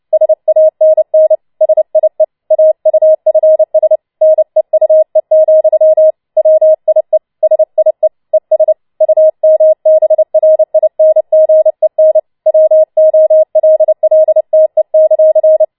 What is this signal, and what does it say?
Morse code, heard as a keyed tone, about 29 wpm, 630 Hz. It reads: SANN SIE AUFS NEUE, WIE SIE ES UMBRINGEN WOLLTE;